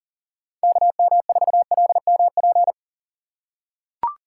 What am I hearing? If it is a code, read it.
Morse code, heard as a keyed tone, about 40 wpm, 700 Hz. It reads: KM4LMP
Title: USA Call Signs - 6 Characters - 6x Character Spacing - 40wpm